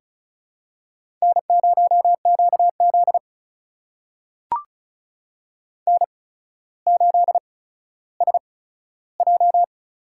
Morse code, heard as a keyed tone, about 36 wpm, spaced as for 15 wpm, 700 Hz. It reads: N0QZ N8SJ
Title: USA Call Signs - 4 Characters - 8x Character Spacing - 35wpm